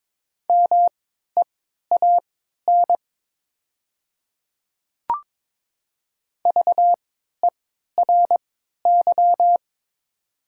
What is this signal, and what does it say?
Morse code, heard as a keyed tone, about 22 wpm, spaced as for 12 wpm, 700 Hz. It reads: MEAN VERY
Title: 3x Character Spacing - Top 200 Words - 22wpm